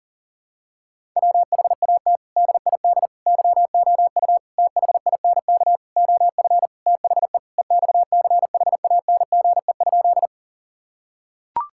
Morse code, heard as a keyed tone, about 37 wpm, 700 Hz. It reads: WHAT DID YOU THINK OF THE EXCHANGE?
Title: Sentences from Top 1000 Words - Review Part 6 - 40wpm